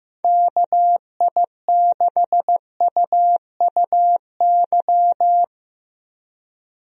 Morse code, heard as a keyed tone, about 15 wpm, 700 Hz. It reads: KI6UUY
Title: USA Call Signs - 6 Characters - 15wpm